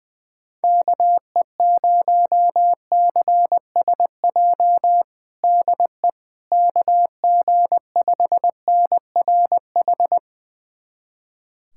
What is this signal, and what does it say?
Morse code, heard as a keyed tone, about 19 wpm, 700 Hz. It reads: KE0CSJ DE KG5NRH
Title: USA Callsign DE Callsign - 6 Characters - 20wpm